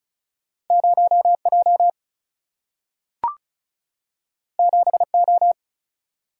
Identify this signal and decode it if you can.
Morse code, heard as a keyed tone, about 36 wpm, 700 Hz. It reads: 0J 7O
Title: Single-Letter Single-Number 35wpm